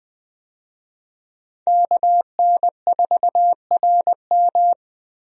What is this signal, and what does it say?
Morse code, heard as a keyed tone, about 20 wpm, 700 Hz. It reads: KN4RM
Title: USA Call Signs - 5 Characters - 20wpm